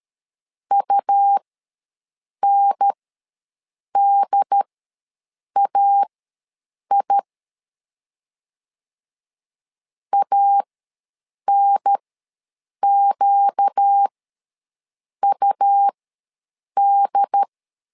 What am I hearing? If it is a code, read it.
Morse code, heard as a keyed tone, about 13 wpm, spaced as for 7 wpm, 800 Hz. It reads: UNDAI ANQUD
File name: LEZIONI / 4 Q ws 12wpm 1912